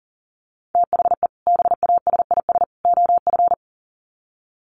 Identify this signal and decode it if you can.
Morse code, heard as a keyed tone, about 40 wpm, 700 Hz. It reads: THE BASIS OF
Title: Sets of 3 Words 40wpm